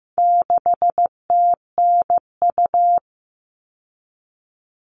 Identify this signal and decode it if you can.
Morse code, heard as a keyed tone, about 15 wpm, 700 Hz. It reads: 6TNU
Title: All Cut Numbers - 4-Digit 15wpm